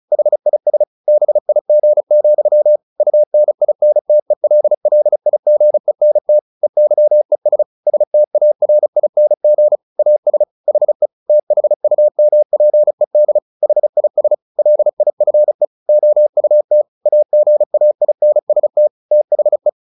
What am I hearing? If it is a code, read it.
Morse code, heard as a keyed tone, about 34 wpm, 600 Hz. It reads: HIS BIG, UNINTELLIGENT EYES STARING AS HE THUMPED HIS LIFE OUT AGAINST THE
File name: the old man and the sea / the old man and the sea 1 35wpm